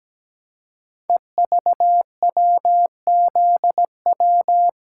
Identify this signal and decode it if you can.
Morse code, heard as a keyed tone, about 17 wpm, 700 Hz. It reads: EVWZW
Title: Code Groups - 5 Letter:Numbers - 17wpm